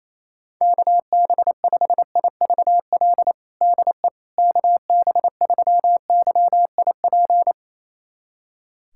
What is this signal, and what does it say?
Morse code, heard as a keyed tone, about 28 wpm, 700 Hz. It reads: KB5IVL DE KB3YIP
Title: USA Callsign DE Callsign - 6 Characters - 28wpm